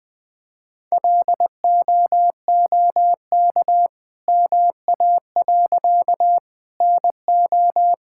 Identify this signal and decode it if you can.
Morse code, heard as a keyed tone, about 20 wpm, 700 Hz. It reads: LOOK MA. NO
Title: All-Age Appropriate Jokes - 20wpm